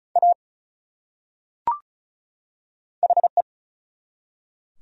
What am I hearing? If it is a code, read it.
Morse code, heard as a keyed tone, about 34 wpm, 700 Hz. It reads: A HE
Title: Top 100 Words 35wpm